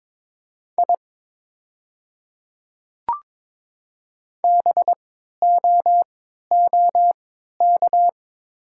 Morse code, heard as a keyed tone, about 22 wpm, spaced as for 12 wpm, 700 Hz. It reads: I BOOK
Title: 3x Character Spacing - Top 300 Words - 22wpm